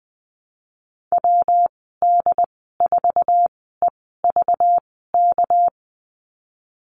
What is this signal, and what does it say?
Morse code, heard as a keyed tone, about 20 wpm, spaced as for 14 wpm, 700 Hz. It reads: WD4EVK
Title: 219 Lesson 35g - Number 0 - Call Signs 20wpm